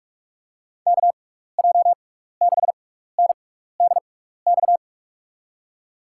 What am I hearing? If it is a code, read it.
Morse code, heard as a keyed tone, about 46 wpm, spaced as for 16 wpm, 700 Hz. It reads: KJ6NDX
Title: USA Call Signs - 6 Characters - 6x Character Spacing - 45wpm